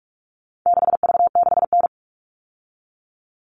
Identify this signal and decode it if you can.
Morse code, heard as a keyed tone, about 44 wpm, 700 Hz. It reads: 6V6N
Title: All Cut Numbers - 4-Digit 45wpm